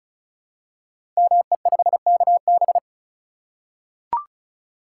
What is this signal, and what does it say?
Morse code, heard as a keyed tone, about 36 wpm, 700 Hz. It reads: ME5KB
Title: Code Groups - 5 Letter:Numbers - 35wpm